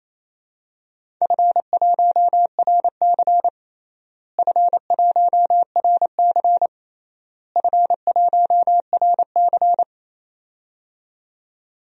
Morse code, heard as a keyed tone, about 28 wpm, 700 Hz. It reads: F1RC F1RC F1RC
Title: DXCC-100 Call Signs - 4 Characters - Repeated 3x - 28wpm